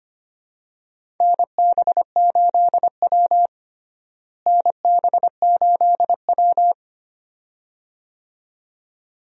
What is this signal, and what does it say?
Morse code, heard as a keyed tone, about 25 wpm, 700 Hz. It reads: NB8W NB8W
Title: USA Call Signs - 4 Characters - Repeated 2x - 25wpm